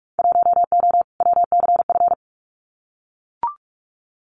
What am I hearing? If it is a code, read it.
Morse code, heard as a keyed tone, about 44 wpm, 700 Hz. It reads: 1O WCF
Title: Winter Field Day Exchanges 45wpm